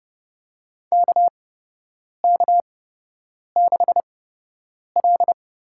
Morse code, heard as a keyed tone, about 30 wpm, spaced as for 8 wpm, 700 Hz. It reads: KK6L
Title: USA Call Signs - 8x Character Spacing - 30wpm